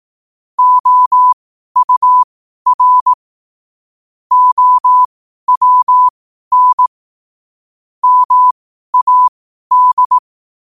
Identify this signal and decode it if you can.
Morse code, heard as a keyed tone, about 18 wpm, spaced as for 12 wpm, 1000 Hz. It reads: OUR OWN MAD
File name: qotdfiles / 12wpm